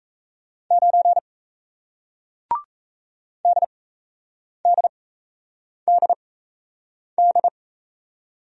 Morse code, heard as a keyed tone, about 37 wpm, spaced as for 8 wpm, 700 Hz. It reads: 9 DDDD